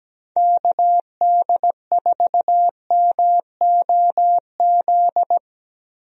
Morse code, heard as a keyed tone, about 17 wpm, 700 Hz. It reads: KD4MOZ
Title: USA Call Signs - 6 Characters - 17wpm